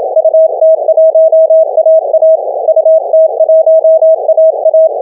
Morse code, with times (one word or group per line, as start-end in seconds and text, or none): 0.2-2.4 UT1AA
2.7-4.9 UT1AA